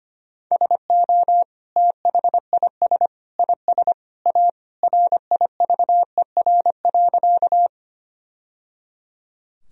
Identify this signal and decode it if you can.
Morse code, heard as a keyed tone, about 25 wpm, 700 Hz. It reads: SO THIS IS A RIVER.
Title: Sentences from Top 1000 Words - Review Part 1 - 25wpm